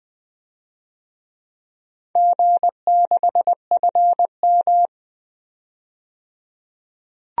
Morse code, heard as a keyed tone, about 20 wpm, 700 Hz. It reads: G6FM
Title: DXCC-100 Call Signs - 4 Characters - 20wpm